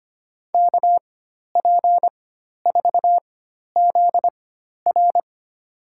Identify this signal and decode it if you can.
Morse code, heard as a keyed tone, about 25 wpm, spaced as for 12 wpm, 700 Hz. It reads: KP4ZR
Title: USA Call Signs - 5 Characters - 4x Character Spacing - 25wpm